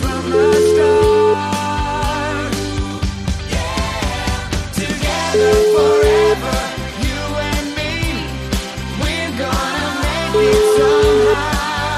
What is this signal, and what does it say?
Ringback tone (Europe, 425 Hz), with noise.